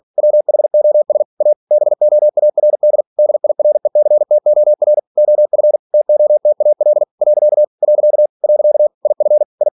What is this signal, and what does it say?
Morse code, heard as a keyed tone, about 46 wpm, 600 Hz. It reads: WHOS A BOARD DIRECTOR OF TOTAL . . IF I